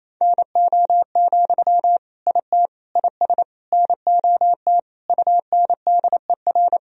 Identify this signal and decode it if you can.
Morse code, heard as a keyed tone, about 28 wpm, 700 Hz. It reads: NO, IT IS NOT UNDER
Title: Sentences from Top 1000 Words - Review Part 1 - 28wpm